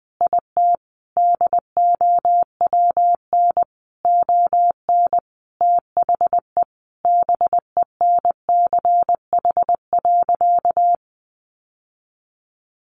Morse code, heard as a keyed tone, about 20 wpm, 700 Hz. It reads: IT DOWN ON THE BENCH.
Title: Sentences from Top 1400 Words 20wpm